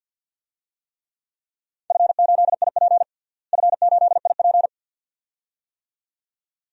Morse code, heard as a keyed tone, about 49 wpm, 700 Hz. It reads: F8IP F8IP
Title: DXCC-100 Call Signs - 4 Characters - Repeated 2x - 50wpm